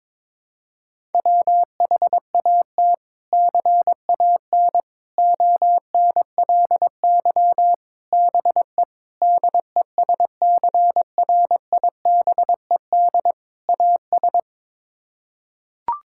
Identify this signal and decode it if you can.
Morse code, heard as a keyed tone, about 22 wpm, 700 Hz. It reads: WHAT CAN ONLY BE DESCRIBED AS
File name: practice / 22wpm / Sets of 6 Words - 22wpm